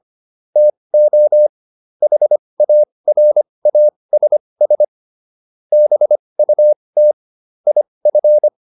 Morse code, heard as a keyed tone, about 25 wpm, spaced as for 20 wpm, 600 Hz. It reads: TO HARASS BUT IF
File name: SS209 20